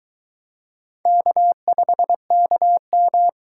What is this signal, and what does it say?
Morse code, heard as a keyed tone, about 22 wpm, 700 Hz. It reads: K5KM